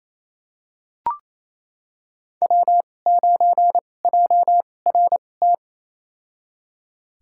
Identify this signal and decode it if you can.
Morse code, heard as a keyed tone, about 28 wpm, spaced as for 20 wpm, 700 Hz. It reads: W9JRT
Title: USA Call Signs - 5 Characters - 2x Character Spacing - 28wpm